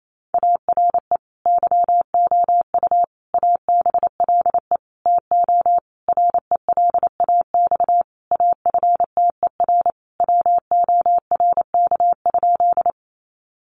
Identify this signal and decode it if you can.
Morse code, heard as a keyed tone, about 28 wpm, 700 Hz. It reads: ARE YOU ABLE TO RELAX AFTER WORK?
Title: Sentences from Top 1400 Words 28wpm